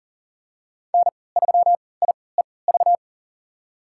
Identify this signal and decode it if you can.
Morse code, heard as a keyed tone, about 40 wpm, spaced as for 23 wpm, 700 Hz. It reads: N3IEV
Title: USA Call Signs - 5 Characters - 3x Character Spacing - 40wpm